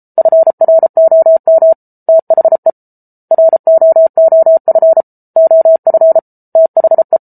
Morse code, heard as a keyed tone, about 34 wpm, 650 Hz. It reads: FROM THE ROOF OF THE